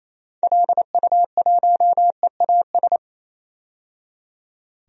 Morse code, heard as a keyed tone, about 28 wpm, 700 Hz. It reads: LU1EAS
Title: DXCC-100 Call Signs - 6 Characters - 28wpm